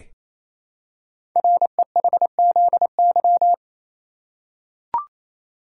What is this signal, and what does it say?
Morse code, heard as a keyed tone, about 27 wpm, 700 Hz. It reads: REHZY
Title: Code Groups - 5 Letters - 28wpm